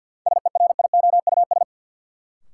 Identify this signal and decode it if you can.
Morse code, heard as a keyed tone, about 52 wpm, 700 Hz. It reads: SERIOUS
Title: Sentences from Top 1100 Words - New Word Drill Alt - 50wpm